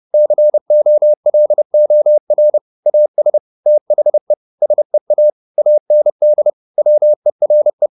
Morse code, heard as a keyed tone, about 30 wpm, 600 Hz. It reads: COLOR AS THE SEA AND WERE
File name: the old man and the sea / the old man and the sea 1 30wpm